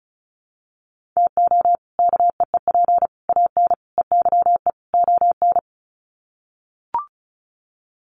Morse code, heard as a keyed tone, about 34 wpm, 700 Hz. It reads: TO KEEP AN EYE ON
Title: Sets of 5 Words Encore 35wpm